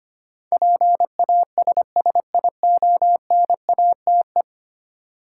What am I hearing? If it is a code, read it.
Morse code, heard as a keyed tone, about 25 wpm, 700 Hz. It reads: PASSIONATE
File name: 10 Letter Words - First 3 Letters Given - 25wpm